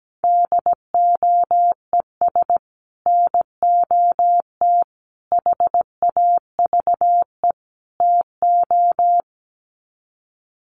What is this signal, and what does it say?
Morse code, heard as a keyed tone, about 17 wpm, 700 Hz. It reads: DOES NOT HAVE TO
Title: Sets of 4 Words 17wpm